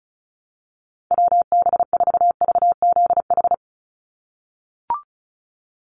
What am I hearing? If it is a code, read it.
Morse code, heard as a keyed tone, about 34 wpm, 700 Hz. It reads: WB4VZH
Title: USA Call Signs 35wpm